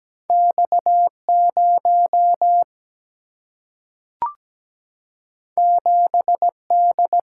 Morse code, heard as a keyed tone, about 17 wpm, 700 Hz. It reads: X0 7D
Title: Single-Letter Single-Number 17wpm